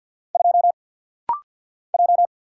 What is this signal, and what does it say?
Morse code, heard as a keyed tone, about 50 wpm, 700 Hz. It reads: 2 J
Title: Single Letter-Number Alt - Rapid-Fire 50wpm